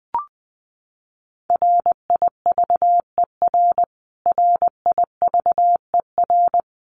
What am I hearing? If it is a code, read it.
Morse code, heard as a keyed tone, about 20 wpm, 700 Hz. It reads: RIVER RIVER
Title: Top 500 Words Repeated 20wpm